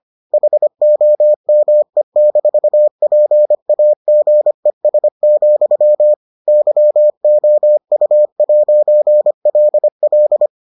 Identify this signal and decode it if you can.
Morse code, heard as a keyed tone, about 25 wpm, 600 Hz. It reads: HOME-PAGES, YOU'LL